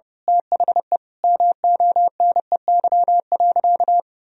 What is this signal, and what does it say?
Morse code, heard as a keyed tone, about 30 wpm, 700 Hz. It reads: THE MONEY.
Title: Sentences from Top 1000 Words - Review Part 2 - 30wpm